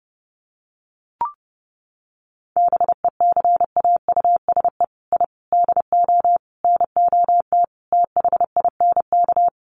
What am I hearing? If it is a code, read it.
Morse code, heard as a keyed tone, about 30 wpm, 700 Hz. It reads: BECAUSE I DO NOT THINK
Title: Sets of 5 Words Encore 30wpm